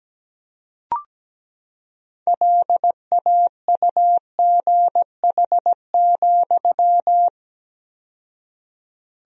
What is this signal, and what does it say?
Morse code, heard as a keyed tone, about 17 wpm, 700 Hz. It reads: LAUGH,
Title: 4x Word-Spacing - Sentences from Top 500 Words - 17wpm